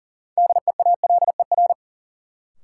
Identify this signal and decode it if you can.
Morse code, heard as a keyed tone, about 39 wpm, 700 Hz. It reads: DEALER
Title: Sentences from Top 1100 Words - New Word Drill Alt - 40wpm